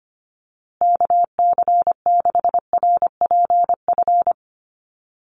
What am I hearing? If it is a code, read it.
Morse code, heard as a keyed tone, about 25 wpm, 700 Hz. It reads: KC6RPF
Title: USA Call Signs Repeated 3x 25wpm